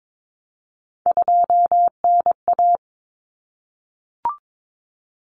Morse code, heard as a keyed tone, about 22 wpm, 700 Hz. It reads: 2NA
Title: Common Cut Numbers - 3-Digit 22wpm